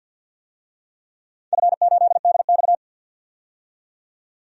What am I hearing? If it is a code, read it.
Morse code, heard as a keyed tone, about 50 wpm, 700 Hz. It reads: F8DX
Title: DXCC-100 Call Signs - 4 Characters - 50wpm